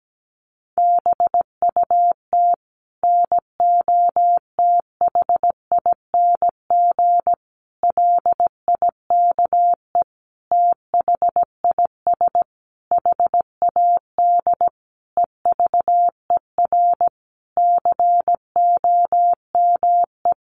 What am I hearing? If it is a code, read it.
Morse code, heard as a keyed tone, about 17 wpm, 700 Hz. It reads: BUT NOTHING LIKE THIS HAD EVER COME
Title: Sentences from Top 1100 Words 17wpm